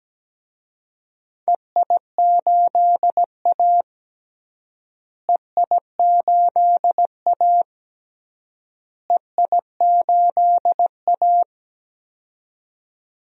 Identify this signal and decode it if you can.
Morse code, heard as a keyed tone, about 17 wpm, 700 Hz. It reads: EI8A EI8A EI8A
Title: DXCC-100 Call Signs - 4 Characters - Repeated 3x - 17wpm